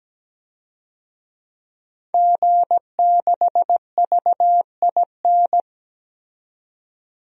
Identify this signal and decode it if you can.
Morse code, heard as a keyed tone, about 17 wpm, 700 Hz. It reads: G6VIN
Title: DXCC-100 Call Signs - 5 Characters - 17wpm